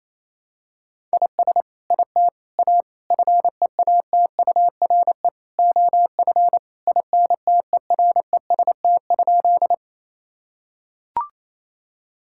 Morse code, heard as a keyed tone, about 28 wpm, 700 Hz. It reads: IS IT A FEATURE OF INTEREST?
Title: Sentences from Top 1000 Words - Review Part 5 - 28wpm